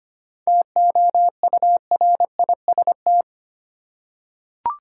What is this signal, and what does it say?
Morse code, heard as a keyed tone, about 25 wpm, 700 Hz. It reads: TOURIST
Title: Seven Letter Words - 25wpm